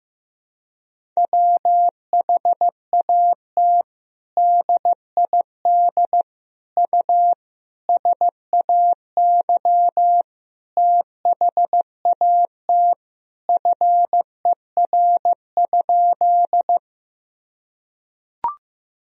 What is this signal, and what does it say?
Morse code, heard as a keyed tone, about 15 wpm, 700 Hz. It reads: WHAT DID U SAY THAT FER?
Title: Sentences from Top 200 Words and Abbreviations 15wpm